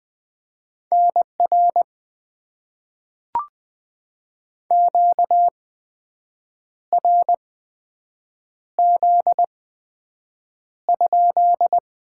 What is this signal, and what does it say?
Morse code, heard as a keyed tone, about 20 wpm, spaced as for 7 wpm, 700 Hz. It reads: NR QRZ?
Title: Top 90 QSO Elements - 8x Character Spacing - 20wpm